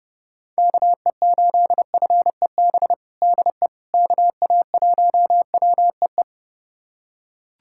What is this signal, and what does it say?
Morse code, heard as a keyed tone, about 30 wpm, 700 Hz. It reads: KE8FEB DE KA1WEE
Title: USA Callsign DE Callsign - 6 Characters - 30wpm